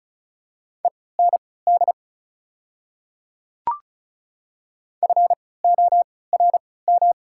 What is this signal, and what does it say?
Morse code, heard as a keyed tone, about 36 wpm, spaced as for 20 wpm, 690 Hz. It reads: END FORM
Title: 3x Character Spacing - Top 500 Words - 35wpm